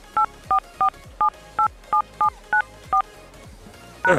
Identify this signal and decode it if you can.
DTMF, heard as telephone touch tones, tones of about 100 ms. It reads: 844787794 #